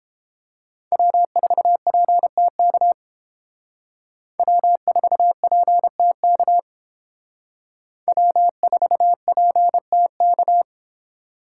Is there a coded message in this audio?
Yes: Morse code from 0.9 to 10.6 s, heard as a keyed tone, about 31 wpm, 700 Hz. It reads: W4PTK W4PTK W4PTK